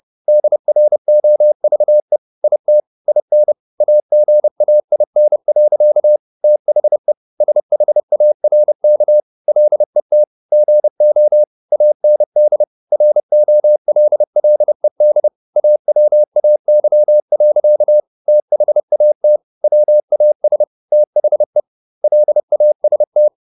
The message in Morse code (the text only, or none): DROVE IT IN AGAIN. THE SHARK LET GO AND ROLLED AWAY. THAT WAS THE LAST